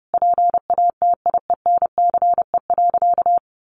Morse code, heard as a keyed tone, about 30 wpm, 700 Hz. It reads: PATIENCE.
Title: Sentences from Top 1300 Words 30wpm